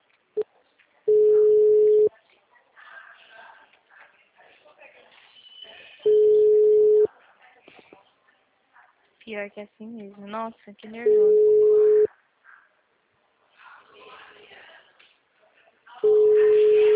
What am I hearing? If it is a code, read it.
Ringback tone (Europe, 425 Hz), with speech.
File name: out- -2021- - - .8610